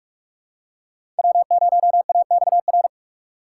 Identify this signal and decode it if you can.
Morse code, heard as a keyed tone, about 45 wpm, 700 Hz. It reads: W0AXR
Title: USA Call Signs - 4x Character Spacing - 45wpm